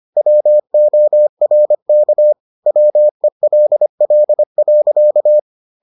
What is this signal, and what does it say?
Morse code, heard as a keyed tone, about 25 wpm, 600 Hz. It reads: WORK WELL.